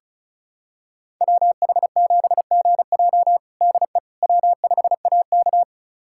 Morse code, heard as a keyed tone, about 36 wpm, 700 Hz. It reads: WH7GJ DE W5AK
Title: USA Callsign DE Callsign 35wpm